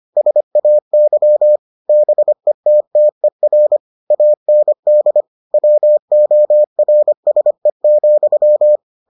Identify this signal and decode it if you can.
Morse code, heard as a keyed tone, about 25 wpm, 600 Hz. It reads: SAY BETTER AND WORSE,